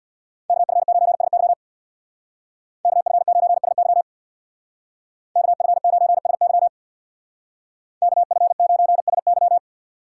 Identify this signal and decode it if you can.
Morse code, heard as a keyed tone, about 64 wpm, 700 Hz. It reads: KF9SY KF9SY KF9SY KF9SY